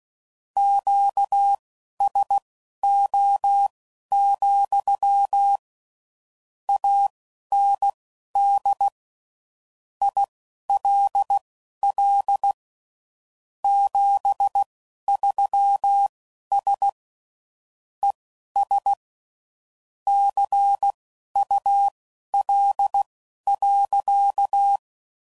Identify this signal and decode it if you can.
Morse code, heard as a keyed tone, about 16 wpm, spaced as for 11 wpm, 800 Hz. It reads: QSO, AND ILL 73S ES CUL.
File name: CW Code Groups @10WPM / QSOs / QSO kc7sac